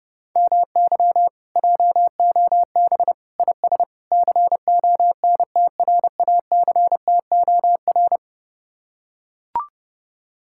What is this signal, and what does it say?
Morse code, heard as a keyed tone, about 29 wpm, 700 Hz. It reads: MY JOB IS CONTRACTOR